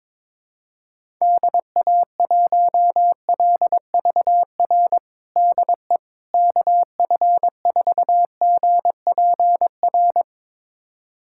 Morse code, heard as a keyed tone, about 22 wpm, 700 Hz. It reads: DA1LVR DE KF4GPR
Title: USA Callsign DE Callsign - 6 Characters - 22wpm